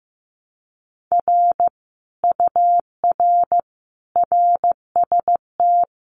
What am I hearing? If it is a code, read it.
Morse code, heard as a keyed tone, about 15 wpm, 700 Hz. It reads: R UR RST
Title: Common QSO Phrases Encore 15wpm